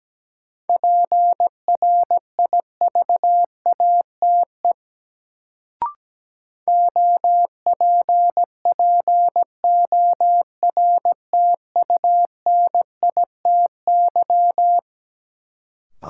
Morse code, heard as a keyed tone, about 17 wpm, 700 Hz. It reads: PRIVATE OPPORTUNITY
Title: Sentences from Top 1000 Words - New Word Drill - 17wpm